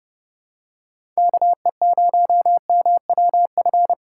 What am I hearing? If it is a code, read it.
Morse code, heard as a keyed tone, about 31 wpm, 700 Hz. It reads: KE0MWF
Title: USA Call Signs - 6 Characters - 4x Character Spacing - 30wpm